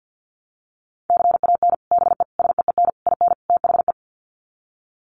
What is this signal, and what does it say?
Morse code, heard as a keyed tone, about 49 wpm, 700 Hz. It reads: CAN BE SEEN IN THE
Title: Sets of 5 Words Encore2 50wpm